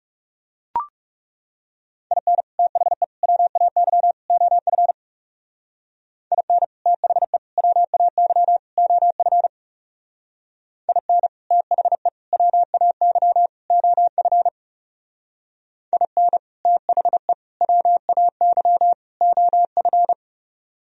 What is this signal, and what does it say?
Morse code, heard as a keyed tone, about 36 wpm, 700 Hz. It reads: IN THE WAY OF IN THE WAY OF IN THE WAY OF IN THE WAY OF
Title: Speed-Racing - Sets of 4 Words 30wpm